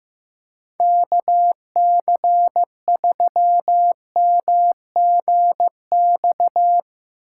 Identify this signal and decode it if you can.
Morse code, heard as a keyed tone, about 15 wpm, 700 Hz. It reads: KC3MGX
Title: USA Call Signs - 6 Characters - 8x Character Spacing - 15wpm